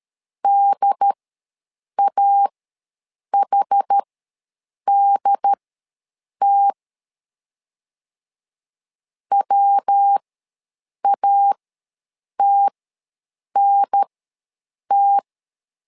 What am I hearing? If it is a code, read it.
Morse code, heard as a keyed tone, about 13 wpm, spaced as for 7 wpm, 800 Hz. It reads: DAHDT WATNT
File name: LEZIONI / 3 MWRO / 3 MW ws 12wpm 1816